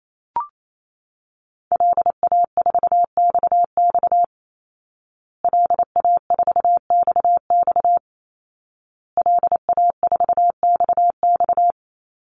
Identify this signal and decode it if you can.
Morse code, heard as a keyed tone, about 28 wpm, 700 Hz. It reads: LA4XX LA4XX LA4XX
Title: Call Signs Repeated 2x - 28wpm